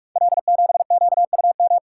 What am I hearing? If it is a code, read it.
Morse code, heard as a keyed tone, about 44 wpm, 690 Hz. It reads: L7QUM
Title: Code Groups - 5 Letter:Numbers - 45wpm